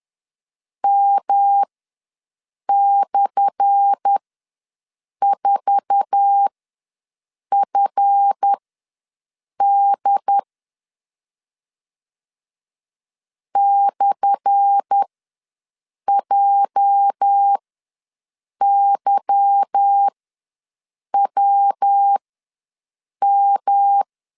Morse code, heard as a keyed tone, about 10 wpm, spaced as for 6 wpm, 800 Hz. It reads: M/4FD /JYWM
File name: LEZIONI / 10 SIGNS / 10A SLASH ws 10wpm 4121